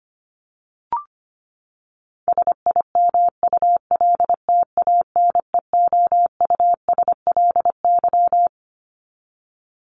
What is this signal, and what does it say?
Morse code, heard as a keyed tone, about 25 wpm, 700 Hz. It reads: SIMULTANEOUSLY
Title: LY Suffix - Top 100 Words with Base Words 25wpm